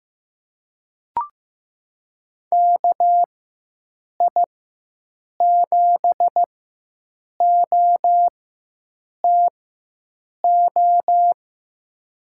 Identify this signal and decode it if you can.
Morse code, heard as a keyed tone, about 15 wpm, spaced as for 7 wpm, 700 Hz. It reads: KI7OTO
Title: USA Call Signs - 6 Characters - 4x Character Spacing - 15wpm